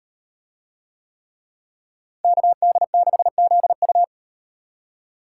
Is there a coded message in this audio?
Yes: Morse code from 2.2 to 4.0 s, heard as a keyed tone, about 36 wpm, 700 Hz. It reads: KD6ZU